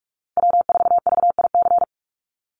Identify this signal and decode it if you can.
Morse code, heard as a keyed tone, about 44 wpm, 700 Hz. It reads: W4VIC
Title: Call Signs Repeated 2x - 45wpm